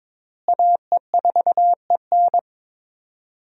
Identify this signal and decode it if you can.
Morse code, heard as a keyed tone, about 22 wpm, 700 Hz. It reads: AE4EN
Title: USA Call Signs - 5 Characters - Repeated 3x - 22wpm